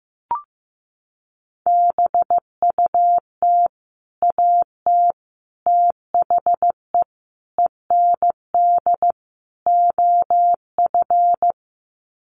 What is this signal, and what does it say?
Morse code, heard as a keyed tone, about 15 wpm, 700 Hz. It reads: BUT AT THE END OF
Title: Sets of 5 Words Encore2 15wpm